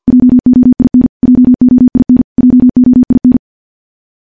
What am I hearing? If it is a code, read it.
Morse code, heard as a keyed tone, about 50 wpm, 250 Hz. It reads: JOIN JOIN JOIN
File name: morse / morse-training-50WPM